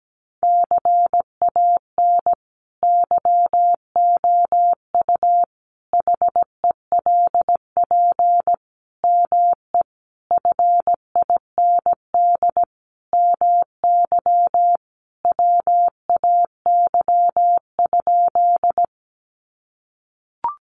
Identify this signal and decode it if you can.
Morse code, heard as a keyed tone, about 17 wpm, 700 Hz. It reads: CAN YOU HELP ME FIND MY WAY?
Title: Sentences from Top 200 Words 17wpm